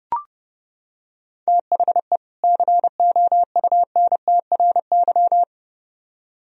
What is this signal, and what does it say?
Morse code, heard as a keyed tone, about 30 wpm, 700 Hz. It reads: THE COUNTRY